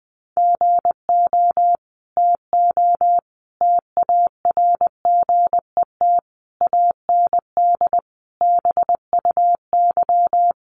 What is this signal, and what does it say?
Morse code, heard as a keyed tone, about 20 wpm, 700 Hz. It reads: GO TO TARGET AND BUY
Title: Sentences from Top 1200 Words 20wpm